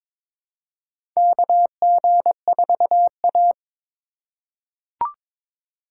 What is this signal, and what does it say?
Morse code, heard as a keyed tone, about 23 wpm, 700 Hz. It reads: KG4A
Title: USA Call Signs - 4 Characters - 4x Character Spacing - 22wpm